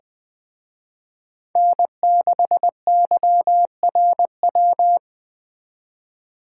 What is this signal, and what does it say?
Morse code, heard as a keyed tone, about 21 wpm, 700 Hz. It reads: N6YRW